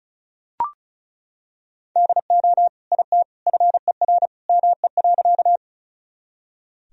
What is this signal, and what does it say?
Morse code, heard as a keyed tone, about 36 wpm, 700 Hz. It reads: DO IT FER ME.
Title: Sentences from Top 200 Words and Abbreviations 35wpm